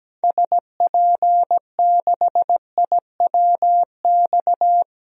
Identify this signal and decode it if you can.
Morse code, heard as a keyed tone, about 17 wpm, 700 Hz. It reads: SP6IWX
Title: DXCC-100 Call Signs - 6 Characters - 17wpm